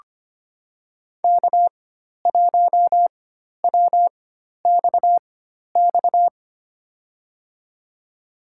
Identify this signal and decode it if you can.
Morse code, heard as a keyed tone, about 25 wpm, spaced as for 12 wpm, 700 Hz. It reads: K1WXX
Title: USA Call Signs - 4x Character Spacing - 25wpm